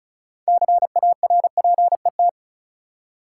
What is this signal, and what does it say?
Morse code, heard as a keyed tone, about 34 wpm, 700 Hz. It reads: CARPET
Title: Six Letter Words - 35wpm